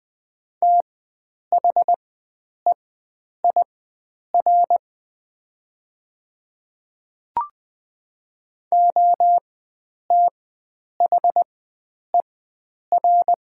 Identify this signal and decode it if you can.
Morse code, heard as a keyed tone, about 20 wpm, spaced as for 9 wpm, 700 Hz. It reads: THEIR OTHER
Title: 4x Character Spacing - Top 500 Words - 20wpm